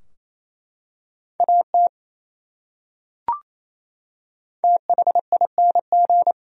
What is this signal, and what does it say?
Morse code, heard as a keyed tone, about 28 wpm, 700 Hz. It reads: AT THING
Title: Top 100 Words Repeated 28wpm